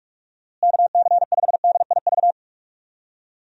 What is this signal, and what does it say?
Morse code, heard as a keyed tone, about 44 wpm, 700 Hz. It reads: KC5DIV
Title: USA Call Signs - 6 Characters - Repeated 2x - 45wpm